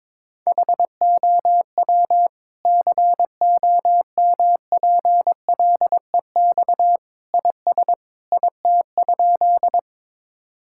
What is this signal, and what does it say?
Morse code, heard as a keyed tone, about 22 wpm, 700 Hz. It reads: HOW COMPLEX IS IT?
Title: Sentences from Top 1000 Words - Review Part 6 - 22wpm